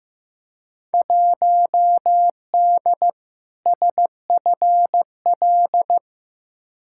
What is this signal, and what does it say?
Morse code, heard as a keyed tone, about 15 wpm, 700 Hz. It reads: 1D SFL
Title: Field Day Exchanges - 15wpm